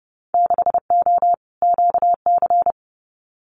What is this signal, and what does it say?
Morse code, heard as a keyed tone, about 32 wpm, 700 Hz. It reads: 6O QC
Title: Winter Field Day Exchanges Encore 30wpm